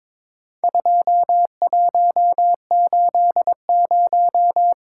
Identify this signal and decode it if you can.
Morse code, heard as a keyed tone, about 23 wpm, 700 Hz. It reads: 2180
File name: practice / 22wpm / Four Letter-Number Combinations - 22wpm